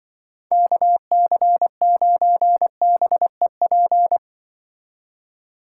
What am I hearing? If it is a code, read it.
Morse code, heard as a keyed tone, about 24 wpm, 700 Hz. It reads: KC9BEP
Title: Speed Racing - USA Call Signs 20wpm